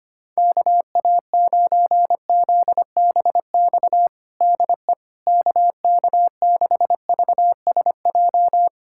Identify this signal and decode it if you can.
Morse code, heard as a keyed tone, about 25 wpm, 700 Hz. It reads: KA9ZBX DE KK6VSJ
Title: USA Callsign DE Callsign - 6 Characters - 25wpm